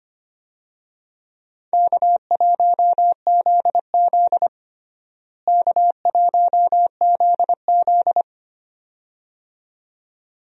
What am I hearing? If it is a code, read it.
Morse code, heard as a keyed tone, about 25 wpm, 700 Hz. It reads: K1ZZ K1ZZ
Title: USA Call Signs - 4 Characters - Repeated 2x - 25wpm